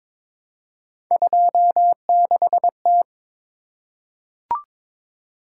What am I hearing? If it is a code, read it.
Morse code, heard as a keyed tone, about 23 wpm, 700 Hz. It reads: 26T
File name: practice / 22wpm / Three Letter-Number Combinations - 22wpm